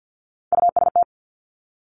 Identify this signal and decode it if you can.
Morse code, heard as a keyed tone, about 50 wpm, 700 Hz. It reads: UST
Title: Top Three Letter Combinations in Words 50wpm